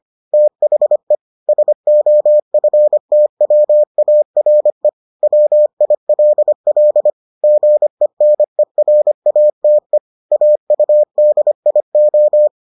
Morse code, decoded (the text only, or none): THE SOFTWARE WILL GENERATE AUDIO